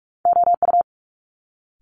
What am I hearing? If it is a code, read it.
Morse code, heard as a keyed tone, about 46 wpm, 700 Hz. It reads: OU
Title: Multi-letter Phonemes Quick Fire 45wpm